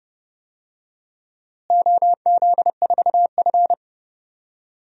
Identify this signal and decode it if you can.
Morse code, heard as a keyed tone, about 30 wpm, 700 Hz. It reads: OZ4F
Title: DXCC-100 Call Signs - 4 Characters - Repeated 3x - 30wpm